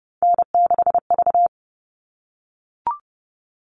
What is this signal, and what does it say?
Morse code, heard as a keyed tone, about 30 wpm, 700 Hz. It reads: N6V
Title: All Cut Numbers - 3-Digit 30wpm